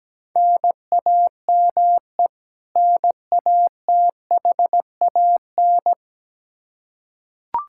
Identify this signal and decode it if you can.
Morse code, heard as a keyed tone, about 17 wpm, 700 Hz. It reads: NAME NATHAN
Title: Practice QSO - Exchange 1 - Beginning - 17wpm